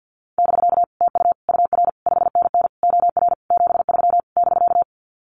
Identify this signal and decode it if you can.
Morse code, heard as a keyed tone, about 49 wpm, 700 Hz. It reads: <BK> TU UR 5NN OR 73 <BK>
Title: Common QSO Phrases Encore 50wpm